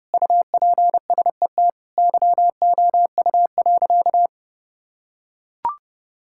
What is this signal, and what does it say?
Morse code, heard as a keyed tone, about 30 wpm, 700 Hz. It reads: UPSET YOU.
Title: Sentences from Top 1000 Words - Review Part 5 - 30wpm